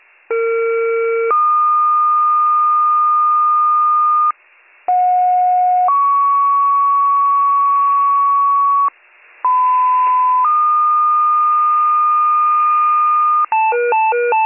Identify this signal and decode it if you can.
Two-tone page, heard as a one-second tone, then a three-second tone, about 460 then 1220 Hz, about 730 then 1120 Hz, about 1010 then 1220 Hz. It is heard from 0.3 to 13.5 s.